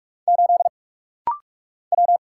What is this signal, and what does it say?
Morse code, heard as a keyed tone, about 44 wpm, 700 Hz. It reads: 8 W
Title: Single Letter-Number Alt - Rapid-Fire 45wpm